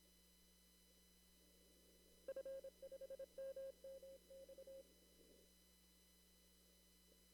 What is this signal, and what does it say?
Morse code, heard as a keyed tone, about 27 wpm, 540 Hz. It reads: F5MMX
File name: CW / kt5j wpx cw-op-k5pi